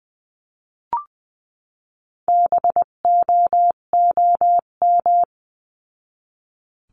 Morse code, heard as a keyed tone, about 20 wpm, 700 Hz. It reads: BOOM
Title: 258 Lesson 42c - Words 16-20wpm